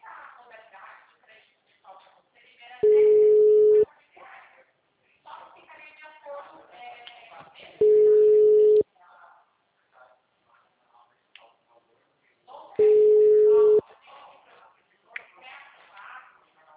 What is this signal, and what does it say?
Ringback tone (Europe, 425 Hz), with speech.
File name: out- -2011- - - .854